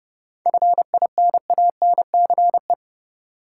Morse code, heard as a keyed tone, about 30 wpm, 700 Hz. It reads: FINANCE
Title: Seven Letter Words - 30wpm